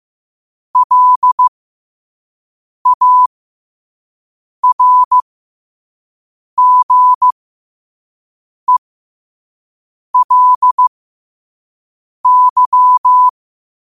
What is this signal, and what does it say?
Morse code, heard as a keyed tone, about 15 wpm, spaced as for 5 wpm, 1000 Hz. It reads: LARGELY
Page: New quotes every day in morse code at 5 Words per minute.